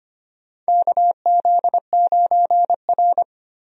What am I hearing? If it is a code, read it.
Morse code, heard as a keyed tone, about 25 wpm, 700 Hz. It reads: KZ9R
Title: USA Call Signs - 4 Characters - Repeated 3x - 25wpm